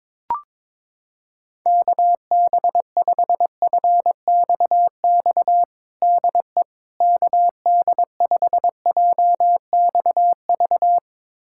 Morse code, heard as a keyed tone, about 22 wpm, 700 Hz. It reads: KB5FXX DE KD5JXV
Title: USA Callsign DE Callsign - 6 Characters - 22wpm